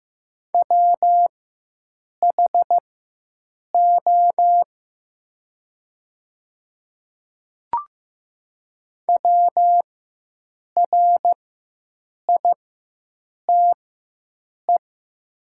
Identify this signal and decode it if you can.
Morse code, heard as a keyed tone, about 15 wpm, spaced as for 7 wpm, 700 Hz. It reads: WHO WRITE
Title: 4x Character Spacing - Top 200 Words - 15wpm